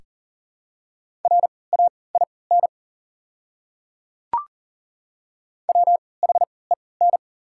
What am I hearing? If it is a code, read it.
Morse code, heard as a keyed tone, about 40 wpm, spaced as for 23 wpm, 700 Hz. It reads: RAIN WHEN
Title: 3x Character Spacing - Top 300 Words - 40wpm